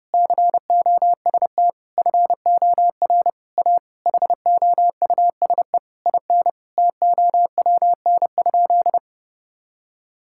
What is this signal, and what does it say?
Morse code, heard as a keyed tone, about 32 wpm, 700 Hz. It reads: COST FOR A HOUSE IN TOWN?
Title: Sentences from Top 1000 Words - Review Part 6 - 30wpm